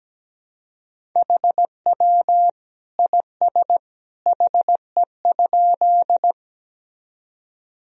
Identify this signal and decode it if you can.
Morse code, heard as a keyed tone, about 17 wpm, 700 Hz. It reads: HW IS HE?
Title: Sentences from Top 100 Words and Abbreviations 17wpm